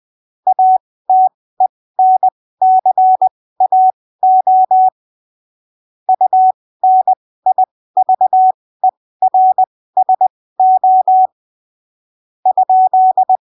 Morse code, heard as a keyed tone, about 20 wpm, spaced as for 15 wpm, 750 Hz. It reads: ATENCAO UNIVERSO ?
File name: arquivos / texto / texto 17 20ppm